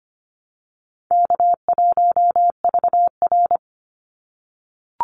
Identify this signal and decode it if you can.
Morse code, heard as a keyed tone, about 25 wpm, 700 Hz. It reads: K1VR
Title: Easy Call Signs - 25wpm